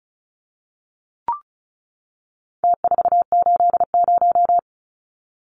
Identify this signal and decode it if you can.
Morse code, heard as a keyed tone, about 34 wpm, 700 Hz. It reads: T480
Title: Numbers - Contest Style - 4-Digit 35wpm